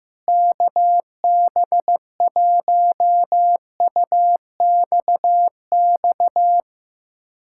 Morse code, heard as a keyed tone, about 15 wpm, 700 Hz. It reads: KB1UXX
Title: USA Callsign DE Callsign - 6 Characters - 15wpm